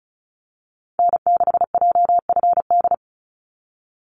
Morse code, heard as a keyed tone, about 34 wpm, 700 Hz. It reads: N6JFD
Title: NA SOTA Activator Callsigns 35wpm